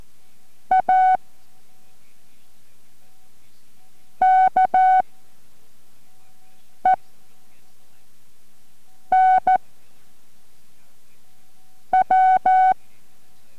Morse code, heard as a keyed tone, about 14 wpm, spaced as for 3 wpm, 750 Hz. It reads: AKEN W